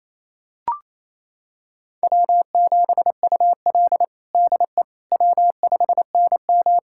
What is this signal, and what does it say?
Morse code, heard as a keyed tone, about 28 wpm, 700 Hz. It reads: W7UL DE W5NM
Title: USA Callsign DE Callsign - 4 Characters - 28wpm